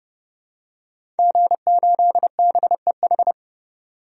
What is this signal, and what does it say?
Morse code, heard as a keyed tone, about 32 wpm, 700 Hz. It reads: G8BEH
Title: DXCC-100 Call Signs - 5 Characters - 30wpm